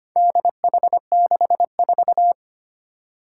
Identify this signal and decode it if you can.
Morse code, heard as a keyed tone, about 25 wpm, 700 Hz. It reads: DH64
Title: Four Challenging Letter-Number Combinations 25wpm